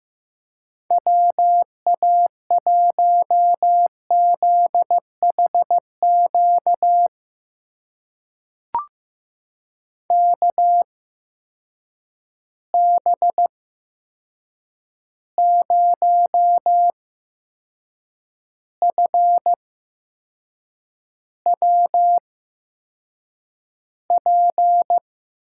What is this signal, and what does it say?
Morse code, heard as a keyed tone, about 15 wpm, spaced as for 6 wpm, 700 Hz. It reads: WA1ZHQ KB0FWP
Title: USA Call Signs - 6 Characters - 8x Character Spacing - 15wpm